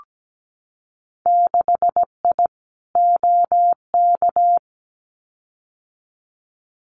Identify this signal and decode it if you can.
Morse code, heard as a keyed tone, about 17 wpm, 700 Hz. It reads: 6I OK
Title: Winter Field Day Exchanges Encore 17wpm